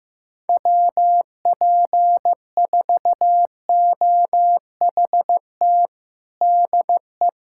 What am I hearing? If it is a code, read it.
Morse code, heard as a keyed tone, about 16 wpm, 700 Hz. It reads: WP4OHT DE
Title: USA Callsign DE Callsign 15wpm